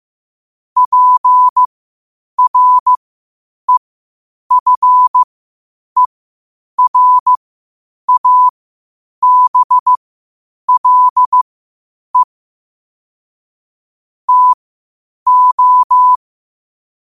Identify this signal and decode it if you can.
Morse code, heard as a keyed tone, about 15 wpm, spaced as for 8 wpm, 1000 Hz. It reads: PREFERABLE TO
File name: qotdfiles / 8wpm